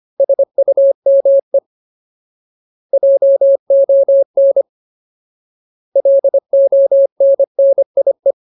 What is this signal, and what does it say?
Morse code, heard as a keyed tone, about 25 wpm, 550 Hz. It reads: SUME JON LONNIE
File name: Ultimate Word List / 25WPM Ultimate